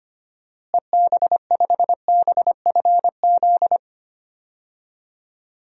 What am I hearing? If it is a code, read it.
Morse code, heard as a keyed tone, about 25 wpm, 700 Hz. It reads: EB5BFZ
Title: DXCC-100 Call Signs - 6 Characters - Repeated 2x - 25wpm